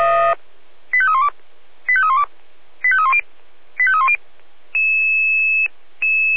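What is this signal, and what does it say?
Five-tone selective call (ZVEI), heard as quick runs of five tones, tones of about 70 ms. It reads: 85312 85312 85310 85310
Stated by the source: ZVEI five-tone selective call received by SDR.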